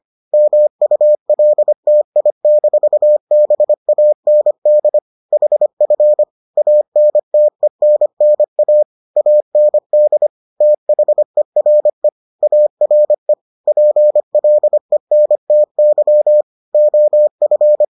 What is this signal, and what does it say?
Morse code, heard as a keyed tone, about 25 wpm, 600 Hz. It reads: MULTI-BAND HF ANTENNA AND THERE ARE PLENTY OF